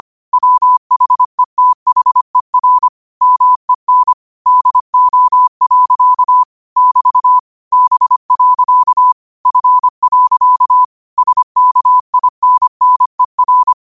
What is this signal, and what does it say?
Morse code, heard as a keyed tone, about 25 wpm, 1000 Hz. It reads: WHETHER MEN DO. <BT> B. F. SKINNER